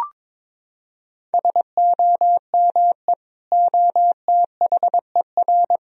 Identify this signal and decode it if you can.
Morse code, heard as a keyed tone, about 22 wpm, 700 Hz. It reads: SOME OTHER
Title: Sentences from Top 1000 Words - Review Part 4 - 22wpm